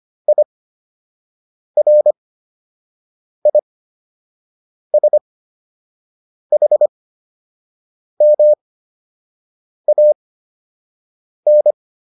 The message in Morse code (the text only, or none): IRISHMAN